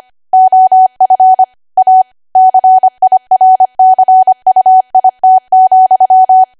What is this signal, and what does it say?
Morse code, heard as a keyed tone, about 25 wpm, 750 Hz. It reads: OF A CIRCUIT,
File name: Archive / 25 WPM / 25WPM